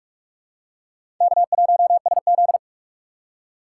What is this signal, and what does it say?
Morse code, heard as a keyed tone, about 44 wpm, 700 Hz. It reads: K1SZ
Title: USA Call Signs - 4 Characters - 6x Character Spacing - 45wpm